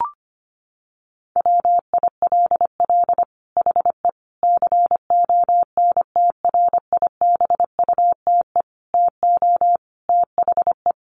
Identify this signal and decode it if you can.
Morse code, heard as a keyed tone, about 25 wpm, 700 Hz. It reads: WILL HE CONTRIBUTE TO THE
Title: Sentences from Top 1400 Words 25wpm